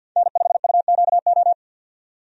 Morse code, heard as a keyed tone, about 50 wpm, 700 Hz. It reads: N5UQO
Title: USA Call Signs - 5 Characters - 50wpm